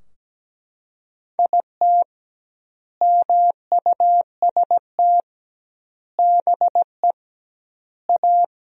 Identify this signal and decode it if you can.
Morse code, heard as a keyed tone, about 17 wpm, 700 Hz. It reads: IT MUST BE A
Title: 2x Word-Spacing - Sentences from Top 500 Words - 17wpm